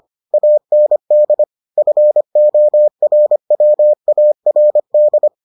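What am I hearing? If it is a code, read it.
Morse code, heard as a keyed tone, about 25 wpm, 600 Hz. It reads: AND FORWARD
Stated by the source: Morse edition of the Foundations of Amateur Radio episodes.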